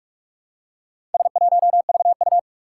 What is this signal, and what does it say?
Morse code, heard as a keyed tone, about 41 wpm, 700 Hz. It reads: S1VU
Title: Four Challenging Letter-Number Combinations 45wpm